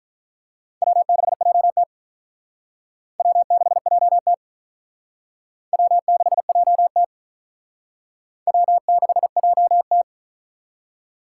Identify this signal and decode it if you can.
Morse code, heard as a keyed tone, about 43 wpm, 700 Hz. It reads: W6JT W6JT W6JT W6JT
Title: Speed Racing - USA Call Signs 35wpm